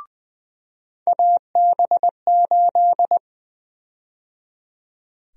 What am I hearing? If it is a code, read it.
Morse code, heard as a keyed tone, about 20 wpm, 700 Hz. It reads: AB8
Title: USA Call Sign Prefixes - 20wpm